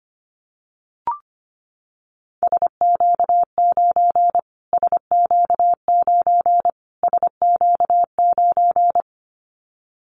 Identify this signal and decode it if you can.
Morse code, heard as a keyed tone, about 25 wpm, 700 Hz. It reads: SQ9 SQ9 SQ9
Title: Callsign Prefixes for Top 25 DXCC - Repeated 3x - 25wpm